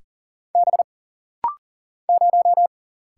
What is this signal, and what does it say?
Morse code, heard as a keyed tone, about 40 wpm, 700 Hz. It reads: B 0
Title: Single Letter-Number - Rapid-Fire 40wpm